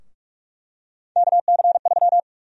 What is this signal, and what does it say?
Morse code, heard as a keyed tone, about 46 wpm, 700 Hz. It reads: KC3
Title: USA Call Sign Prefixes - Repeated 2x - 45wpm